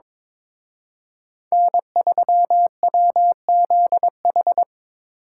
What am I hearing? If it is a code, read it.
Morse code, heard as a keyed tone, about 25 wpm, 700 Hz. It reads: N3WZH
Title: Speed Racing - USA 1x3 Call Signs 22wpm